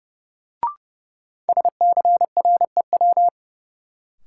Morse code, heard as a keyed tone, about 32 wpm, 700 Hz. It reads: SCREW
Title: Sentences from Top 1600 Words - New Word Drill - 30wpm